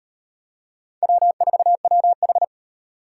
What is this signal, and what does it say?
Morse code, heard as a keyed tone, about 35 wpm, 700 Hz. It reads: W4WH